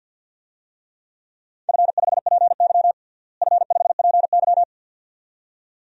Morse code, heard as a keyed tone, about 47 wpm, 700 Hz. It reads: F5PY F5PY
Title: DXCC-100 Call Signs - 4 Characters - Repeated 2x - 50wpm